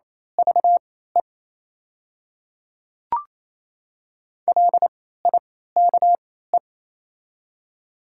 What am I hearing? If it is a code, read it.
Morse code, heard as a keyed tone, about 27 wpm, spaced as for 16 wpm, 690 Hz. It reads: VE LIKE
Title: 3x Character Spacing - Top 500 Words - 28wpm